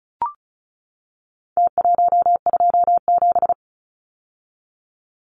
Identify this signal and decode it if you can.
Morse code, heard as a keyed tone, about 34 wpm, 700 Hz. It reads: T127
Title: Numbers - Contest Style - 4-Digit 35wpm